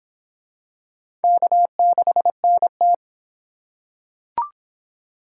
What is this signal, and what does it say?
Morse code, heard as a keyed tone, about 23 wpm, 700 Hz. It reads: K6NT